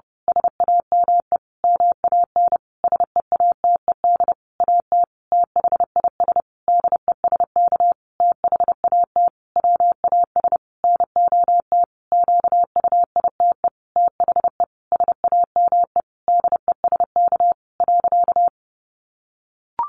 Morse code, heard as a keyed tone, about 30 wpm, 700 Hz. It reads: SAME MAN SEATED AT THIS DESK THAT WAS NOT QUITE THE SAME DESK.
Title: A Sound Of Thunder 30wpm